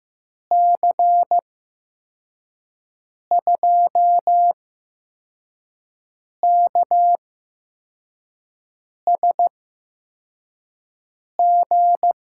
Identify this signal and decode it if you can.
Morse code, heard as a keyed tone, about 15 wpm, spaced as for 4 wpm, 700 Hz. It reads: C2KSG
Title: USA Call Signs - 6 Characters - 8x Character Spacing - 15wpm